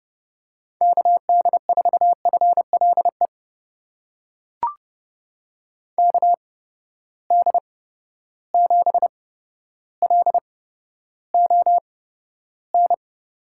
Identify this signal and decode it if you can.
Morse code, heard as a keyed tone, about 32 wpm, spaced as for 13 wpm, 700 Hz. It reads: KD4FLE KD7LON
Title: USA Call Signs - 6 Characters - 8x Character Spacing - 30wpm